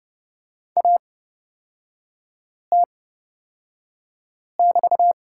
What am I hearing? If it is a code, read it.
Morse code, heard as a keyed tone, about 30 wpm, spaced as for 5 wpm, 700 Hz. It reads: AT<BT>
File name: Lesson 36e - BT - Three Characters 30wpm